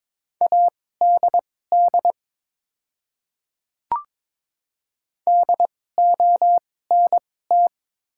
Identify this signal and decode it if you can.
Morse code, heard as a keyed tone, about 23 wpm, spaced as for 16 wpm, 700 Hz. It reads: ADD DONT
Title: 2x Character Spacing - Top 300 Words - 22wpm